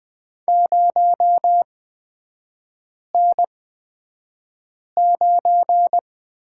Morse code, heard as a keyed tone, about 20 wpm, spaced as for 5 wpm, 700 Hz. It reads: 0N9
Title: 224 Lesson 36e - BT - Three Characters 20wpm